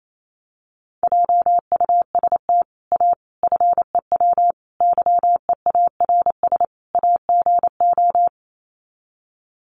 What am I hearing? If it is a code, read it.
Morse code, heard as a keyed tone, about 28 wpm, 700 Hz. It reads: JUST A FEW YEARS AGO
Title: Sets of 5 Words Encore 28wpm